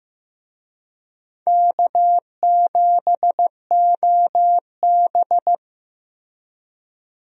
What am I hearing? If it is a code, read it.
Morse code, heard as a keyed tone, about 15 wpm, 700 Hz. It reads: K7OB
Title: USA Call Signs - 4 Characters - Repeated 2x - 15wpm